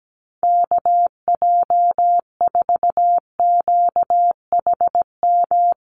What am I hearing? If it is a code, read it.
Morse code, heard as a keyed tone, about 17 wpm, 700 Hz. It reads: KJ4QHM
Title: USA Call Signs Repeated 2x 17wpm